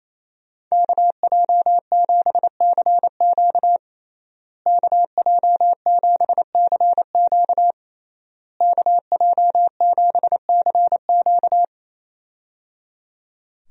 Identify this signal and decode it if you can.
Morse code, heard as a keyed tone, about 28 wpm, 700 Hz. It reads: KJ7CQ KJ7CQ KJ7CQ
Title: USA Call Signs - 5 Characters - Repeated 3x - 28wpm